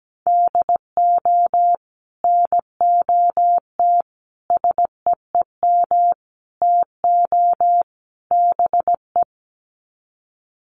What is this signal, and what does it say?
Morse code, heard as a keyed tone, about 17 wpm, 700 Hz. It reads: DO NOT SEEM TO BE
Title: Sets of 5 Words Encore2 17wpm